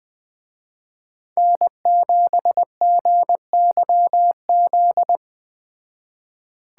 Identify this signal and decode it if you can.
Morse code, heard as a keyed tone, about 20 wpm, 700 Hz. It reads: N7GYZ